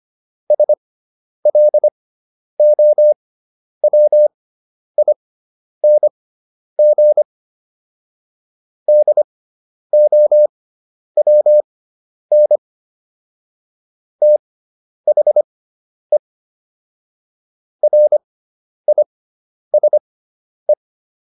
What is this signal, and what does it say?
Morse code, heard as a keyed tone, about 25 wpm, spaced as for 10 wpm, 600 Hz. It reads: SLOWING DOWN THE RISE